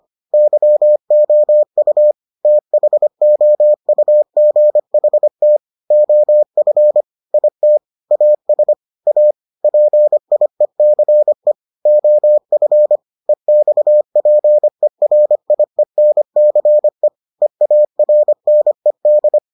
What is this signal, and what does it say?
Morse code, heard as a keyed tone, about 25 wpm, 600 Hz. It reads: YOU THOUGHT OF IT AS A PIECE OF EXPERIENCE EARNED